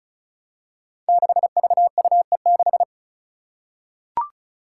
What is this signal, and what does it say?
Morse code, heard as a keyed tone, about 36 wpm, 700 Hz. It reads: 6VUE6
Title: Five Challenging Letter-Number Combinations 35wpm